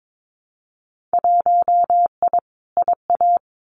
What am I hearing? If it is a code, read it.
Morse code, heard as a keyed tone, about 22 wpm, 700 Hz. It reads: 1I IA
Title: Winter Field Day Exchanges 22wpm